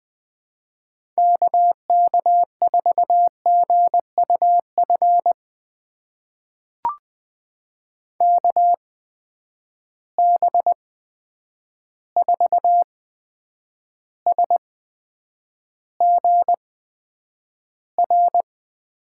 Morse code, heard as a keyed tone, about 20 wpm, spaced as for 9 wpm, 700 Hz. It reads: KK4GUF KB4SGR
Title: USA Call Signs - 6 Characters - 8x Character Spacing - 20wpm